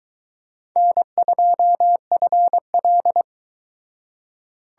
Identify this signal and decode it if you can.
Morse code, heard as a keyed tone, about 22 wpm, 700 Hz. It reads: N2FL